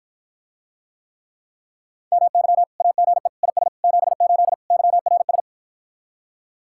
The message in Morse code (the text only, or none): MY AGE IS 78 YRS